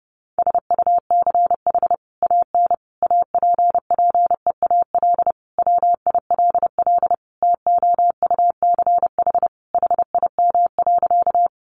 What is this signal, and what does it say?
Morse code, heard as a keyed tone, about 30 wpm, 700 Hz. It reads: SUCH AN APPEAL WILL TOUCH HIM.
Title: Sentences from Top 1300 Words 30wpm